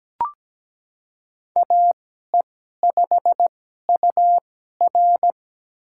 Morse code, heard as a keyed tone, about 17 wpm, spaced as for 12 wpm, 700 Hz. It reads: AE5UR
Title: USA Call Signs - 5 Characters - 2x Character Spacing - 17wpm